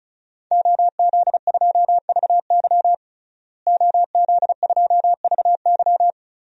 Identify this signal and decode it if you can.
Morse code, heard as a keyed tone, about 36 wpm, 700 Hz. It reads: OZ2VY OZ2VY
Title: DXCC-100 Call Signs - 5 Characters - Repeated 3x - 35wpm